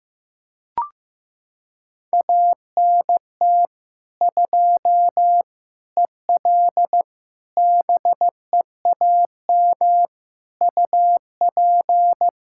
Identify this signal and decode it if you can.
Morse code, heard as a keyed tone, about 16 wpm, 700 Hz. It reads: ANT 2 EL BEAM UP
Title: Practice QSO - Exchange 2 - Info - 15wpm